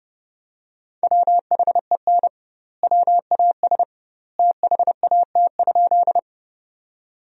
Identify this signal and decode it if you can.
Morse code, heard as a keyed tone, about 32 wpm, 700 Hz. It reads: WHEN WAS THAT?
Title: 2x Word-Spacing - Sentences from Top 100 Words - 30wpm